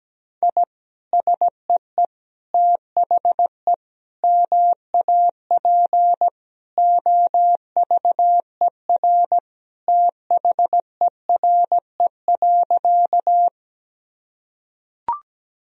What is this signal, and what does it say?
Morse code, heard as a keyed tone, about 17 wpm, 700 Hz. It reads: I SEE THE MAP OVER THERE.
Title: Sentences from Top 1000 Words - Review Part 2 - 17wpm